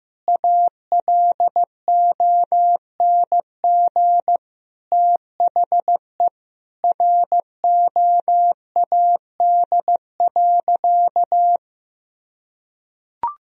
Morse code, heard as a keyed tone, about 15 wpm, 700 Hz. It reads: ALONG THE ROAD.
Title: Sentences from Top 1000 Words - Review Part 2 - 15wpm